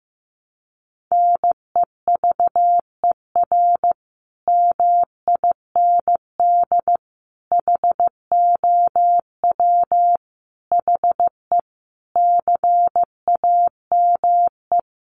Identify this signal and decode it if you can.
Morse code, heard as a keyed tone, about 15 wpm, 700 Hz. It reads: NEVER MIND HOW HE CAME
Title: Sentences from Top 500 Words 15wpm